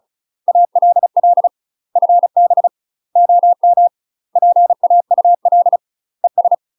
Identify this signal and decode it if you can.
Morse code, heard as a keyed tone, about 35 wpm, 700 Hz. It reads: ALL FB OM PAUL ES
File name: UK MASTER QSOs TRAINING / 35WPM / UK MASTER QSOs 0032